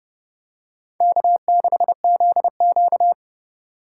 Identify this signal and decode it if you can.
Morse code, heard as a keyed tone, about 30 wpm, 700 Hz. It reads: K6ZQ